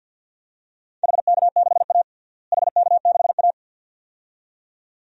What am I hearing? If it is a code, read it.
Morse code, heard as a keyed tone, about 49 wpm, 700 Hz. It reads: HK6A HK6A
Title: DXCC-100 Call Signs - 4 Characters - Repeated 2x - 50wpm